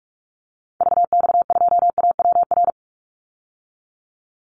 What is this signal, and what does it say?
Morse code, heard as a keyed tone, about 44 wpm, 700 Hz. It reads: VX2AWR
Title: Call Signs Repeated 2x - 45wpm